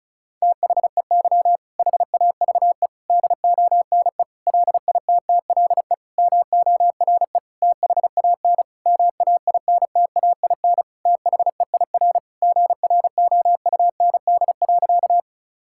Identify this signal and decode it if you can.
Morse code, heard as a keyed tone, about 36 wpm, 700 Hz. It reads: THEY HAVE DONE LITTLE MORE THAN MAINTAIN THEIR GROUND.
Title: Sentences from Top 1000 Words - Review Part 5 - 35wpm